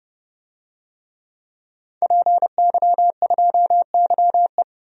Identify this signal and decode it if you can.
Morse code, heard as a keyed tone, about 30 wpm, 700 Hz. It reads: PY2YE
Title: DXCC-100 Call Signs - 5 Characters - Repeated 2x - 30wpm